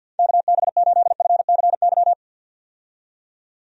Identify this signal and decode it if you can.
Morse code, heard as a keyed tone, about 49 wpm, 700 Hz. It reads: KB8FCY
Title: USA Call Signs - 6 Characters - 50wpm